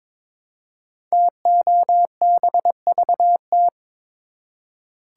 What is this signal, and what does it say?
Morse code, heard as a keyed tone, about 22 wpm, 700 Hz. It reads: TOBVT
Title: Code Groups - 5 Letters - 22wpm